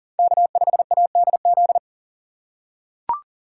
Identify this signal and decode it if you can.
Morse code, heard as a keyed tone, about 40 wpm, 700 Hz. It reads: K5ADZ
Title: USA Call Signs - 5 Characters - 3x Character Spacing - 40wpm